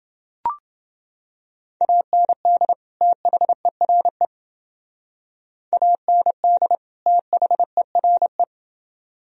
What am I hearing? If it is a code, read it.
Morse code, heard as a keyed tone, about 29 wpm, 700 Hz. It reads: AND THERE AND THERE